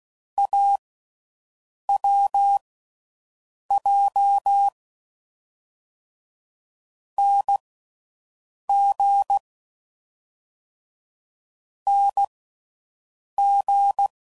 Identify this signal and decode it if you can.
Morse code, heard as a keyed tone, about 16 wpm, spaced as for 6 wpm, 800 Hz. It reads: AWJ NG NG